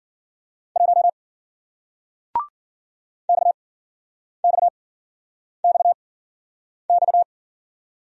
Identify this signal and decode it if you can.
Morse code, heard as a keyed tone, about 50 wpm, 700 Hz. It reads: 1 X X X X
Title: Speed-Racing Single Letter-Number 40wpm